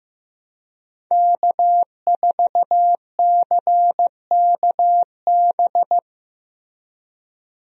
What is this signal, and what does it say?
Morse code, heard as a keyed tone, about 15 wpm, 700 Hz. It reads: K4CKB
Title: USA Call Signs - 5 Characters - Repeated 3x - 15wpm